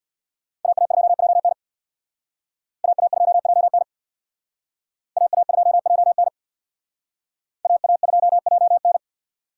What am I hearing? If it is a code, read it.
Morse code, heard as a keyed tone, about 62 wpm, 700 Hz. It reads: AA2JN AA2JN AA2JN AA2JN
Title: Speed Racing - USA Call Signs 50wpm